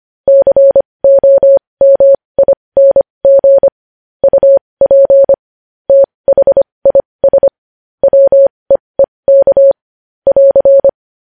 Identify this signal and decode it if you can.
Morse code, heard as a keyed tone, about 25 wpm, spaced as for 20 wpm, 550 Hz. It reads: COMING UP THIS WEEK <AR>
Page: news headlines in morse code at 20 wpm